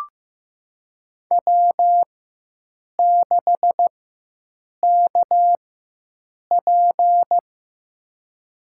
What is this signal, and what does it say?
Morse code, heard as a keyed tone, about 15 wpm, spaced as for 7 wpm, 700 Hz. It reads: W6KP
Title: USA Call Signs - 4 Characters - 4x Character Spacing - 15wpm